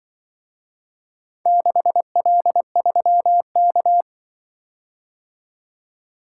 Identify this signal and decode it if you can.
Morse code, heard as a keyed tone, about 24 wpm, 700 Hz. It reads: 6L3K
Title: Speed-Racing - Four Letter-Number Combinations - 20wpm